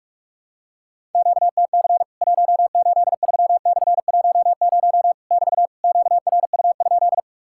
Morse code, heard as a keyed tone, about 44 wpm, 700 Hz. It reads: QTC 183/10 <BT> QRU?
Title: Worked All Europe DX Contest - QTC Exchange - 45wpm